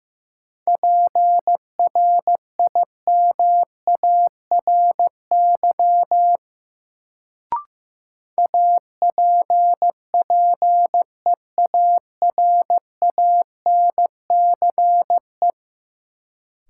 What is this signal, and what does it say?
Morse code, heard as a keyed tone, about 15 wpm, 700 Hz. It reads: PRIMARY APPEARANCE
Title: Sentences from Top 1100 Words - New Word Drill Alt - 15wpm